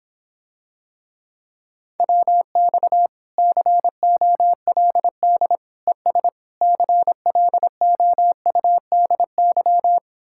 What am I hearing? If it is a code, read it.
Morse code, heard as a keyed tone, about 26 wpm, 700 Hz. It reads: WX COLD ES CLOUDY